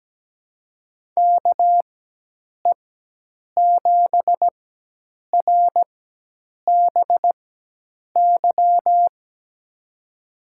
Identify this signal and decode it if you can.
Morse code, heard as a keyed tone, about 17 wpm, spaced as for 8 wpm, 700 Hz. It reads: KE7RBY
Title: USA Call Signs - 4x Character Spacing - 17wpm